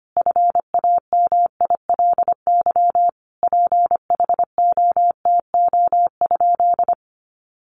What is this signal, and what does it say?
Morse code, heard as a keyed tone, about 25 wpm, 700 Hz. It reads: FAMILY PHOTO?
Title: Sentences from Top 1200 Words 25wpm